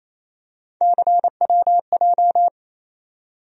Morse code, heard as a keyed tone, about 28 wpm, 700 Hz. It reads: CWJ
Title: USA Call Sign Postfixes - 3 Letters 28wpm